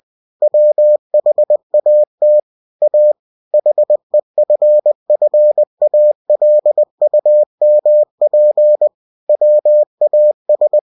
Morse code, heard as a keyed tone, about 20 wpm, 600 Hz. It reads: WHAT A HEFFALUMP WAS
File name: cw-books / winnie-the-pooh / winnie-20wpm-page-0033